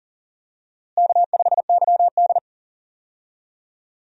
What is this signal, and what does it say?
Morse code, heard as a keyed tone, about 39 wpm, 700 Hz. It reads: K5YD
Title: USA Call Signs - 4 Characters - Repeated 3x - 40wpm